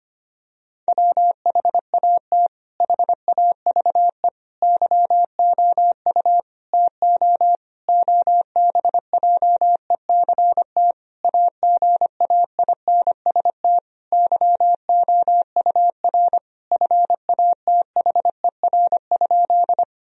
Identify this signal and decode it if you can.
Morse code, heard as a keyed tone, about 25 wpm, 700 Hz. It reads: WHAT HAVE YOU TO OBJECT AGAINST YOUR FATHER?
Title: Sentences from Top 1000 Words - Review Part 3 - 25wpm